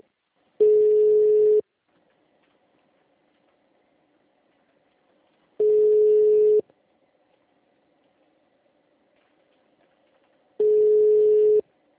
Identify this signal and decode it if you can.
Ringback tone (Europe, 425 Hz), with speech.